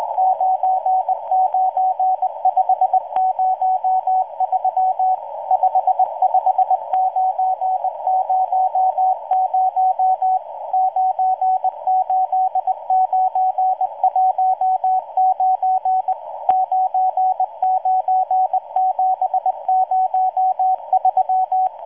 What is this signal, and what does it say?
Morse code, heard as a keyed tone, about 19 wpm, 750 Hz, with radio noise. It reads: 99503 55900 98919 99703